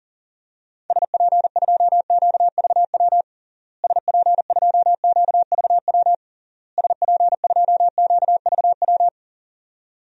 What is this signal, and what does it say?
Morse code, heard as a keyed tone, about 40 wpm, 700 Hz. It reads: SP2QVW SP2QVW SP2QVW
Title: DXCC-100 Call Signs - 6 Characters - Repeated 3x - 40wpm